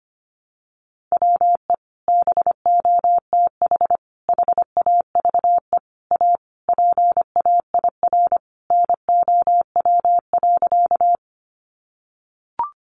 Morse code, heard as a keyed tone, about 25 wpm, 700 Hz. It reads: WE BOTH HAVE A PAIR NOW.
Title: Sentences from Top 1200 Words 25wpm